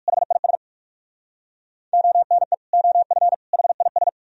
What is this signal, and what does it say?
Morse code, heard as a keyed tone, about 57 wpm, 700 Hz. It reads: HIS ONE OF HIS